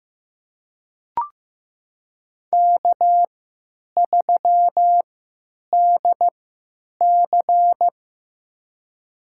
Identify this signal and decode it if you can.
Morse code, heard as a keyed tone, about 15 wpm, spaced as for 8 wpm, 700 Hz. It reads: K3DC
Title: USA Call Signs - 4 Characters - 3x Character Spacing - 15wpm